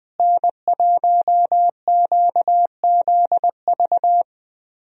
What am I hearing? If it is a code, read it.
Morse code, heard as a keyed tone, about 20 wpm, 700 Hz. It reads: N1QZV
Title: USA Call Signs - 5 Characters - Repeated 2x - 20wpm